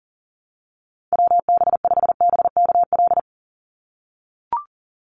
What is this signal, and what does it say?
Morse code, heard as a keyed tone, about 40 wpm, 700 Hz. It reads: WB5BKL
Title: Call Signs - 40wpm